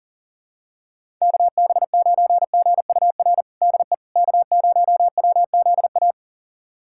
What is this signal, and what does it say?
Morse code, heard as a keyed tone, about 40 wpm, 700 Hz. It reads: KB9GUR DE K0WZA
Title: USA Callsign DE Callsign 40wpm